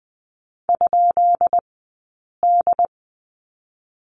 Morse code, heard as a keyed tone, about 20 wpm, spaced as for 8 wpm, 700 Hz. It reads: ?D
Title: 151 Lesson 25d - Letter p - Two Characters 20wpm